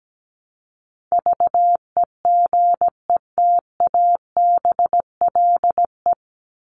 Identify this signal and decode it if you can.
Morse code, heard as a keyed tone, about 17 wpm, 700 Hz. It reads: VEGETABLE
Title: Top 1251-1500 Words 17wpm